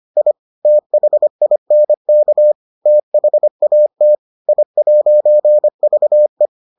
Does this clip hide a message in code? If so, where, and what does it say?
Yes: Morse code from 0.2 to 6.5 s, heard as a keyed tone, about 25 wpm, 600 Hz. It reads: I THINK THAT I'VE